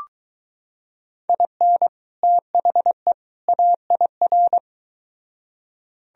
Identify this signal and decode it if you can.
Morse code, heard as a keyed tone, about 22 wpm, 700 Hz. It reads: IN THE AIR